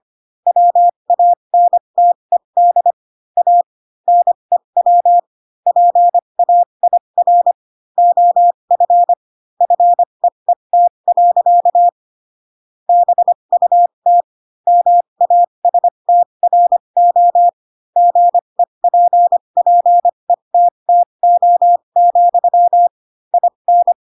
Morse code, heard as a keyed tone, about 25 wpm, spaced as for 22 wpm, 700 Hz. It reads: WANTED A NEW PAIR OF FEET. BUT MASTRO GEPPETTO, IN
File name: Pinocchio / Pinocchio ch8 18 25 ik1hge